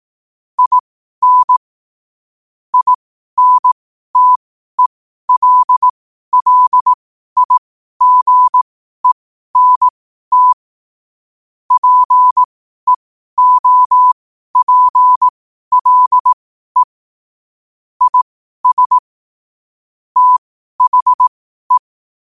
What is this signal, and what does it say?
Morse code, heard as a keyed tone, about 18 wpm, spaced as for 12 wpm, 1000 Hz. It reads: IN INTELLIGENT PEOPLE IS THE
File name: qotdfiles / 12wpm